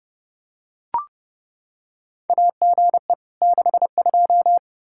Morse code, heard as a keyed tone, about 29 wpm, 700 Hz. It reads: AGE 62
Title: Practice QSO - Exchange 3 - More Info - 30wpm